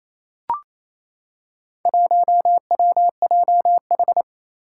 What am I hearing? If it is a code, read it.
Morse code, heard as a keyed tone, about 28 wpm, 700 Hz. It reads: 1WJH
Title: Four Challenging Letter-Number Combinations 28wpm